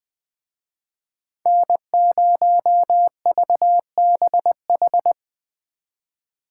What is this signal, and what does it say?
Morse code, heard as a keyed tone, about 20 wpm, 700 Hz. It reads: N0VBH